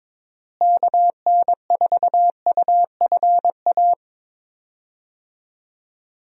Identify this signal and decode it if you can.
Morse code, heard as a keyed tone, about 21 wpm, 700 Hz. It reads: KN4UFA
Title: USA Call Signs - 6 Characters - Repeated 3x - 22wpm